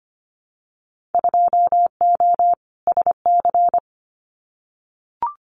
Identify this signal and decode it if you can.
Morse code, heard as a keyed tone, about 25 wpm, 700 Hz. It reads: 2O SC
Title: Winter Field Day Exchanges 25wpm